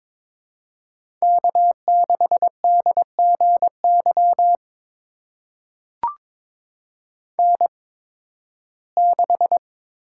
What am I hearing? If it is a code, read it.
Morse code, heard as a keyed tone, about 22 wpm, spaced as for 14 wpm, 700 Hz. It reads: K6DGY N6
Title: USA Call Signs - 5 Characters - 8x Character Spacing - 22wpm